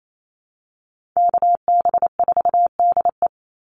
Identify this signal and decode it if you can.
Morse code, heard as a keyed tone, about 28 wpm, 700 Hz. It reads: KB4DE
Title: Call Signs - 28wpm